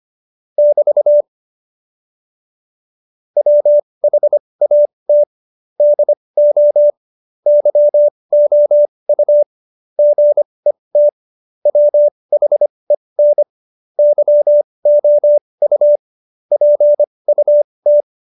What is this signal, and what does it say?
Morse code, heard as a keyed tone, about 25 wpm, spaced as for 20 wpm, 590 Hz. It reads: <BT> WHAT DO YOU GET WHEN YOU PUT
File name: Jokes-25-char-20-eff